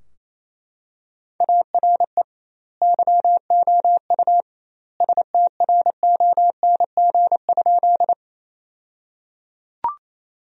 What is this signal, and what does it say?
Morse code, heard as a keyed tone, about 28 wpm, 700 Hz. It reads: ARE YOU STRONG?
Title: 2x Word-Spacing - Sentences from Top 500 Words - 28wpm